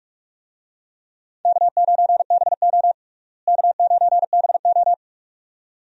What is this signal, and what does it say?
Morse code, heard as a keyed tone, about 46 wpm, 700 Hz. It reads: K9BO K9BO
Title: USA Call Signs - 4 Characters - Repeated 2x - 45wpm